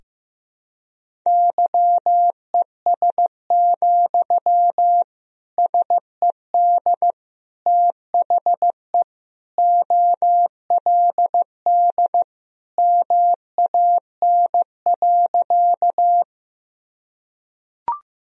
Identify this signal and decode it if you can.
Morse code, heard as a keyed tone, about 16 wpm, 700 Hz. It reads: YES, SED THE OLD MAN.
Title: Sentences from Top 500 Words and Abbreviations 15wpm